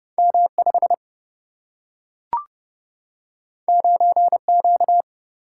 Morse code, heard as a keyed tone, about 31 wpm, 700 Hz. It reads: M 5 9 Q
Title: Single-Letter Single-Number 30wpm